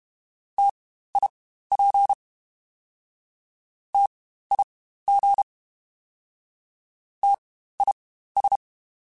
Morse code, heard as a keyed tone, about 32 wpm, spaced as for 15 wpm, 800 Hz. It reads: TIP TIG TIS
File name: Spaced Alphas / Spaced Alphas mp3s / Ts sp @30